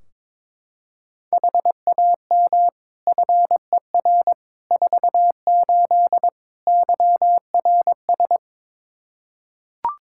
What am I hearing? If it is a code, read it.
Morse code, heard as a keyed tone, about 21 wpm, 700 Hz. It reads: HAM FER 48 YRS
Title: K1USN SST QSO Practice 22wpm